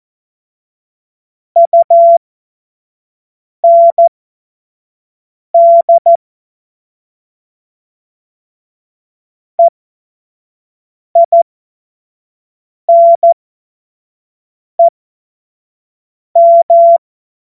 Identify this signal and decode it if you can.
Morse code, heard as a keyed tone, about 14 wpm, spaced as for 5 wpm, 680 Hz. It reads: UND EINEM